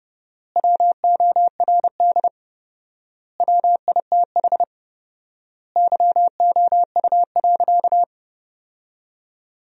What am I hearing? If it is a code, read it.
Morse code, heard as a keyed tone, about 32 wpm, 700 Hz. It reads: WORD WITH YOU.
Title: 4x Word-Spacing - Sentences from Top 300 Words - 30wpm